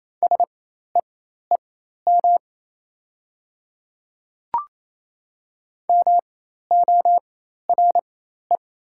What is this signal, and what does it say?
Morse code, heard as a keyed tone, about 28 wpm, spaced as for 13 wpm, 690 Hz. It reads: SEEM MORE
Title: 4x Character Spacing - Top 300 Words - 28wpm